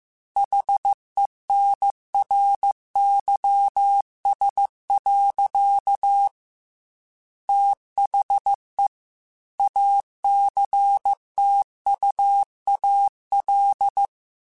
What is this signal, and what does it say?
Morse code, heard as a keyed tone, about 15 wpm, 800 Hz. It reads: HENRYS. THE ACTUAL
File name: Technical Subjects / Technical Subjects-mp3s / XL @13